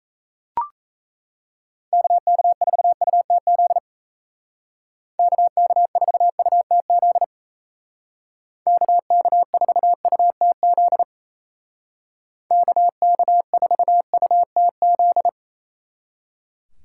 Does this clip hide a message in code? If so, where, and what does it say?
Yes: Morse code from 1.9 to 15.3 s, heard as a keyed tone, about 37 wpm, 700 Hz. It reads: KK4UTZ KK4UTZ KK4UTZ KK4UTZ